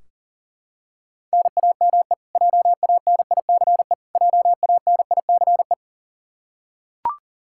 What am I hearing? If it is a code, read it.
Morse code, heard as a keyed tone, about 37 wpm, 700 Hz. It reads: NAME JANICE JANICE
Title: Practice QSO - Exchange 1 - Beginning - 40wpm